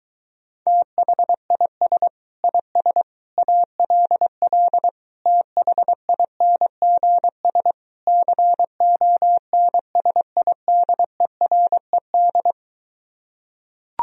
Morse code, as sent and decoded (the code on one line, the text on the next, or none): - .... .. ... / .. ... / .- .-.. .-.. / - .... .. -. --. ... / -.-. --- -. ... .. -.. .-. -..
THIS IS ALL THINGS CONSIDERED